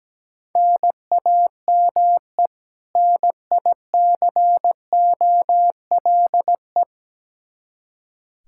Morse code, heard as a keyed tone, about 17 wpm, 700 Hz. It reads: NAME NICOLE
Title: Practice QSO - Exchange 1 - Beginning - 17wpm